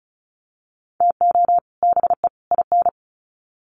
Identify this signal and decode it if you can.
Morse code, heard as a keyed tone, about 34 wpm, 700 Hz. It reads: TO BE IN
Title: Sets of 3 Words 35wpm